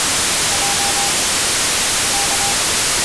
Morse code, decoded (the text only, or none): J K